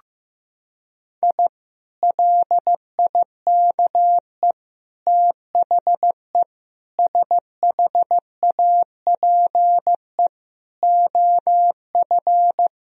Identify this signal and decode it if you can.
Morse code, heard as a keyed tone, about 15 wpm, 700 Hz. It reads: I LIKE THE SHAPE OF
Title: Sentences from Top 1000 Words - Review Part 3 - 15wpm